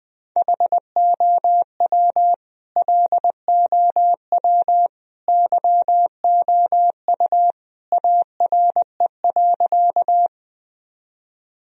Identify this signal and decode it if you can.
Morse code, heard as a keyed tone, about 21 wpm, 700 Hz. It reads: HOW LOW YOU ARE.
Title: Sentences from Top 1000 Words - Review Part 1 - 20wpm